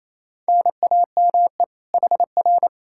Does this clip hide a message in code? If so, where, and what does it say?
Yes: Morse code from 0.5 to 2.7 s, heard as a keyed tone, about 38 wpm, 700 Hz. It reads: NAME HR